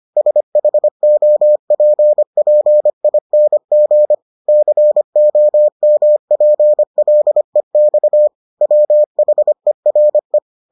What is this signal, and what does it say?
Morse code, heard as a keyed tone, about 25 wpm, 600 Hz. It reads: SHOPPING COMPLEX WHERE